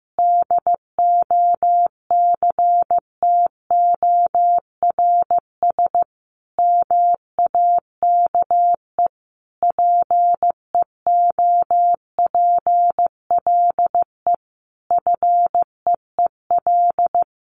Morse code, heard as a keyed tone, about 15 wpm, 700 Hz. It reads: DOCTORS MAKE PEOPLE FEEL
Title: Easy Phrases 15wpm